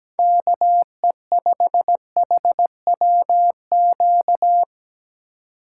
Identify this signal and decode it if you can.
Morse code, heard as a keyed tone, about 20 wpm, 700 Hz. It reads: KE5HWQ
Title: Speed Racing - USA 2x3 Call Signs 17wpm